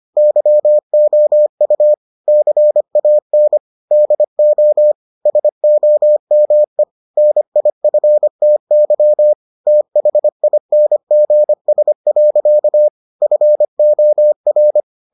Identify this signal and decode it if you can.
Morse code, heard as a keyed tone, about 25 wpm, 600 Hz. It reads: YOU CAN DO SOME NIFTY THINGS. FOR